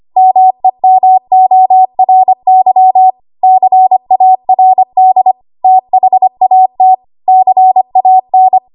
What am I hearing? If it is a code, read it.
Morse code, heard as a keyed tone, about 25 wpm, 750 Hz. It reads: MEMORY CARD THAT CAN
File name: Archive / 25 WPM / 25WPM